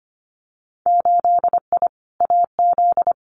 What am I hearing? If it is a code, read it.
Morse code, heard as a keyed tone, about 25 wpm, 700 Hz. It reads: 8I AZ
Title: Winter Field Day Exchanges Encore 25wpm